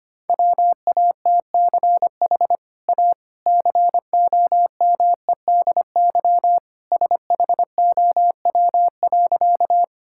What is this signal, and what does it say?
Morse code, heard as a keyed tone, about 25 wpm, 700 Hz. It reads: WATCH A COMEDY SHOW.
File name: Easy Sentences with Hints - 25wpm